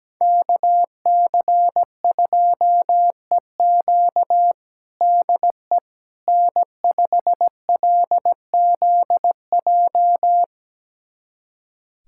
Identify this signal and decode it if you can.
Morse code, heard as a keyed tone, about 17 wpm, 700 Hz. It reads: KC2EQ DE N5LZJ
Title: USA Callsign DE Callsign - 5 Characters - 17wpm